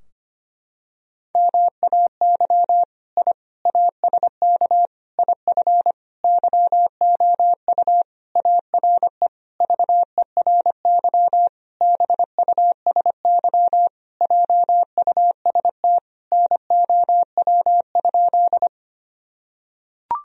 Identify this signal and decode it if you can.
Morse code, heard as a keyed tone, about 25 wpm, 700 Hz. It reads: MAY I ASK IF YOU ARE VERY BUSY JUST NOW?
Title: Sentences from Top 1000 Words - Review Part 3 - 25wpm